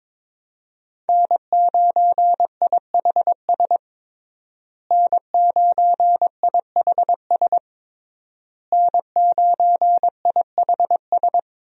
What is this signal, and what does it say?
Morse code, heard as a keyed tone, about 21 wpm, 700 Hz. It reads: N9IHS N9IHS N9IHS
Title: USA Call Signs - 5 Characters - Repeated 3x - 22wpm